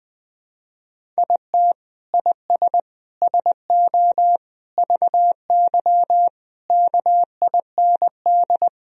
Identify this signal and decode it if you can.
Morse code, heard as a keyed tone, about 20 wpm, 700 Hz. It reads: IT IS SO VY KIND
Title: Sentences from Top 300 Words and Abbreviations 20wpm